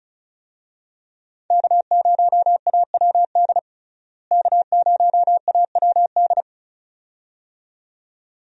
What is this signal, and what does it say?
Morse code, heard as a keyed tone, about 36 wpm, 700 Hz. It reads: K0AWD K0AWD
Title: USA Call Signs - 5 Characters - Repeated 2x - 35wpm